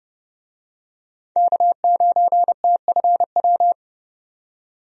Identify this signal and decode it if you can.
Morse code, heard as a keyed tone, about 30 wpm, 700 Hz. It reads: K9TFW
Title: USA Call Signs - 5 Characters - 30wpm